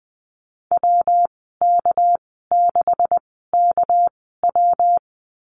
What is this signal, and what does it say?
Morse code, heard as a keyed tone, about 20 wpm, spaced as for 15 wpm, 700 Hz. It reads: WK6KW
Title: 231 Lesson 37g - Letter x - Call Signs 20wpm